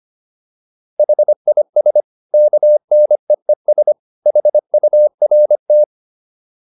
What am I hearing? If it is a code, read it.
Morse code, heard as a keyed tone, about 25 wpm, 600 Hz. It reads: HIS KNEES HURT
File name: PR405 25